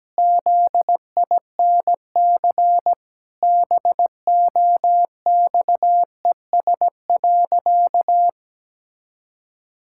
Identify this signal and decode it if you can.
Morse code, heard as a keyed tone, about 17 wpm, 700 Hz. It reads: ZINC BOXES.
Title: Pangram Challenge v2 - 17wpm